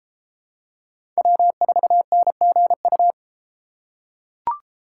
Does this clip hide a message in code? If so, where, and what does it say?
Yes: Morse code from 1.2 to 3.1 s, heard as a keyed tone, about 32 wpm, 700 Hz. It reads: W4NGU